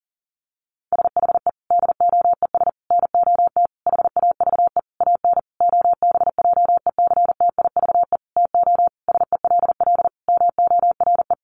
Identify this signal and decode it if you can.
Morse code, heard as a keyed tone, about 40 wpm, 700 Hz. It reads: SHE DOES NOT HAVE AN OBJECTIVE TO SELL MORE
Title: Sentences from Top 1200 Words 40wpm